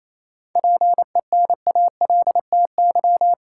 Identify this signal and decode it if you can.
Morse code, heard as a keyed tone, about 28 wpm, 700 Hz. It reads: PENALTY
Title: Seven Letter Words - 28wpm